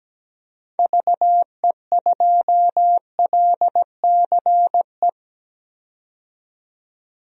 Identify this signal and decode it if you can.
Morse code, heard as a keyed tone, about 17 wpm, 700 Hz. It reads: VE2LCE
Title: DXCC-100 Call Signs - 6 Characters - 17wpm